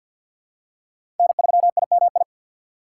Morse code, heard as a keyed tone, about 53 wpm, 700 Hz. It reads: N3IMI
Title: USA Call Signs - 5 Characters - 2x Character Spacing - 50wpm